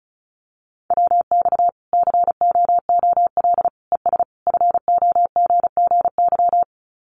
Morse code, heard as a keyed tone, about 34 wpm, 700 Hz. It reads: WX COOL ES FOGGY
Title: Common QSO Phrases 35wpm